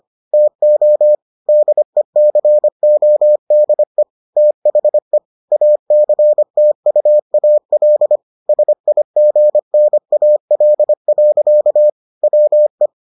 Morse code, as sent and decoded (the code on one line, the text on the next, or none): - --- / -.. -.-. --- -.. / - .... / .- -.-. - ..- .- .-.. / ... .. --. -. .- .-.. .-.-.- / .-- .
TO DECODE THE ACTUAL SIGNAL. WE